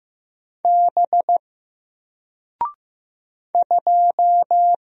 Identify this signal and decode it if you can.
Morse code, heard as a keyed tone, about 15 wpm, 700 Hz. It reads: B 2
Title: Single Character (Letter:Number:Punctuation:Prosign) 15wpm